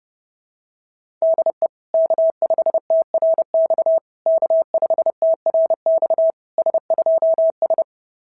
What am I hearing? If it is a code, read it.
Morse code, heard as a keyed tone, about 30 wpm, 690 Hz. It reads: DE K5TRX K5TRX S2S
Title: SOTA QSO Practice 30wpm